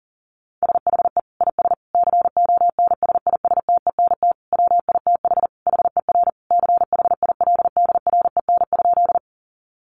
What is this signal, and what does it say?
Morse code, heard as a keyed tone, about 40 wpm, 700 Hz. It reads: SHE IS CONSISTENT WITH HER CHILDREN?
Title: Sentences from Top 1500 Words 40wpm